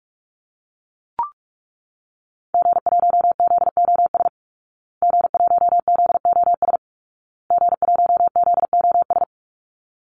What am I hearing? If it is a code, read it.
Morse code, heard as a keyed tone, about 44 wpm, 700 Hz. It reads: G1ZOS G1ZOS G1ZOS
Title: Call Signs Repeated 2x - 45wpm